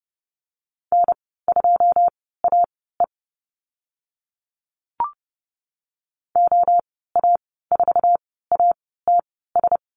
Morse code, heard as a keyed tone, about 29 wpm, spaced as for 17 wpm, 700 Hz. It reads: N2AE OA4ATS
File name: Lesson 15g - Number 5 - Call Signs 30wpm